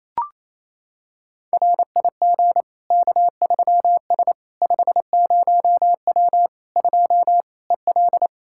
Morse code, heard as a keyed tone, about 28 wpm, 700 Hz. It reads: RIG K3S 50W 2 EL
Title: Practice QSO - Exchange 2 - Info - 28wpm